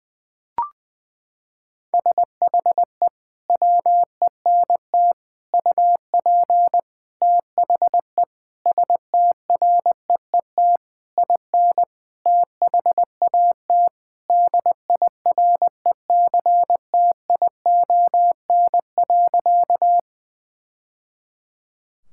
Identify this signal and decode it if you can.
Morse code, heard as a keyed tone, about 21 wpm, 700 Hz. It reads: SHE WENT UP THE STREET IN THAT DIRECTION.
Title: Sentences from Top 1000 Words - Review Part 5 - 20wpm